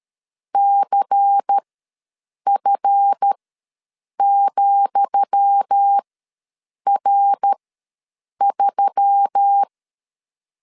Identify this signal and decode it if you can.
Morse code, heard as a keyed tone, about 13 wpm, spaced as for 7 wpm, 800 Hz. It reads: CF,R3
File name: LEZIONI / 10 SIGNS / 10D COMA ws 12wpm 3618